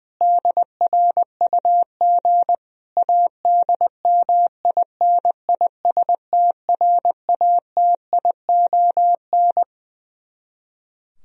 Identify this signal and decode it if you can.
Morse code, heard as a keyed tone, about 20 wpm, 700 Hz. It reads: DRUG ADMINISTRATION
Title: Speed-Racing - Sets of 5 Words 20wpm